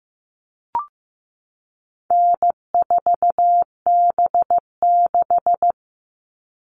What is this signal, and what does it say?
Morse code, heard as a keyed tone, about 15 wpm, 700 Hz. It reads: N4B6
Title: All Cut Numbers - 4-Digit 15wpm